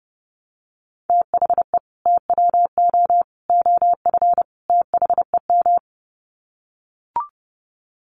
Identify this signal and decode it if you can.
Morse code, heard as a keyed tone, about 30 wpm, 700 Hz. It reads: THE TWO OF THEM
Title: Sets of 4 Words 30wpm